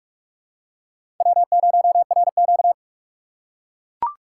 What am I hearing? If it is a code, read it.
Morse code, heard as a keyed tone, about 41 wpm, 700 Hz. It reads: W0RQ
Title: USA Call Signs - 6x Character Spacing - 45wpm